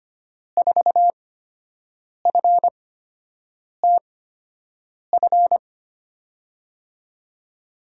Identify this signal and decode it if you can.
Morse code, heard as a keyed tone, about 25 wpm, spaced as for 7 wpm, 700 Hz. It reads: 4FTF
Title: USA Call Signs - 8x Character Spacing - 25wpm